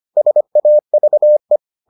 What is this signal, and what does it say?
Morse code, heard as a keyed tone, about 25 wpm, 600 Hz. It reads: SAVE